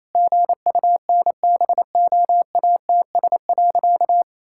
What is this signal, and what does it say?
Morse code, heard as a keyed tone, about 27 wpm, 700 Hz. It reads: GUNBOATS.
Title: Pangram Challenge v2 - 28wpm